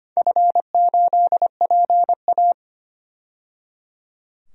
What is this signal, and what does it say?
Morse code, heard as a keyed tone, about 25 wpm, 700 Hz. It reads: F8PA
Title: DXCC-100 Call Signs - 4 Characters - 25wpm